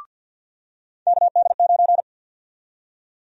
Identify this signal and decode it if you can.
Morse code, heard as a keyed tone, about 52 wpm, 700 Hz. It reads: KD9
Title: USA Call Sign Prefixes - 50wpm